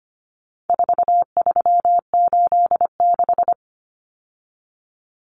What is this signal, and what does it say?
Morse code, heard as a keyed tone, about 25 wpm, 700 Hz. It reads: 4386
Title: Numbers - 4-Digit 25wpm